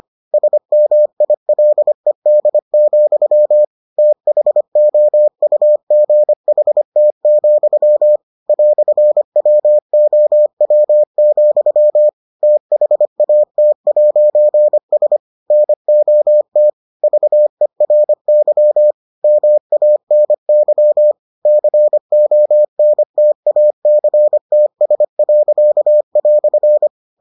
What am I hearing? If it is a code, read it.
Morse code, heard as a keyed tone, about 25 wpm, 600 Hz. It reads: SMILED, THOUGHT, "WOW, THAT'S NOT VERY MANY CONTACTS."